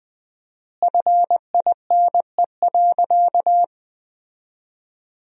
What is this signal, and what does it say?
Morse code, heard as a keyed tone, about 21 wpm, 700 Hz. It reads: FINE.
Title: 4x Word-Spacing - Sentences from Top 500 Words - 20wpm